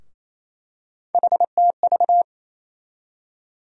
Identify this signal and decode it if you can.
Morse code, heard as a keyed tone, about 28 wpm, 700 Hz. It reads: HTV
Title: USA Call Sign Postfixes - 3 Letters 28wpm